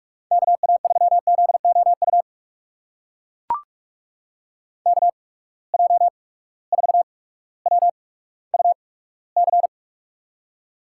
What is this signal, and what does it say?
Morse code, heard as a keyed tone, about 44 wpm, spaced as for 19 wpm, 690 Hz. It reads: KA3ZOU KJ4WUC
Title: USA Call Signs - 8x Character Spacing - 45wpm